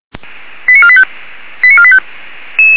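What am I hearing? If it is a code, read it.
Five-tone selective call (ZVEI), heard as quick runs of five tones, tones of about 70 ms. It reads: 89485 89485
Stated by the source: ZVEI five-tone selective call received on an SDR.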